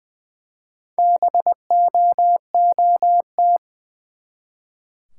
Morse code, heard as a keyed tone, about 20 wpm, 700 Hz. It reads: BOOT
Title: Four Letter Words - 20wpm